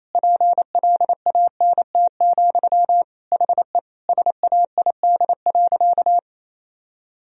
Morse code, heard as a keyed tone, about 28 wpm, 700 Hz. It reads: PLANT, HE SAID.
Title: Sentences from Top 1000 Words - Review Part 1 - 28wpm